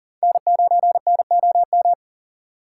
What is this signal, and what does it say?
Morse code, heard as a keyed tone, about 50 wpm, 700 Hz. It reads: N9NOM
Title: Speed Racing - USA 1x3 Call Signs 40wpm